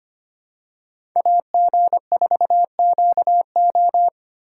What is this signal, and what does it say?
Morse code, heard as a keyed tone, about 25 wpm, 700 Hz. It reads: AG4QO
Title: USA Call Signs - 5 Characters - 2x Character Spacing - 25wpm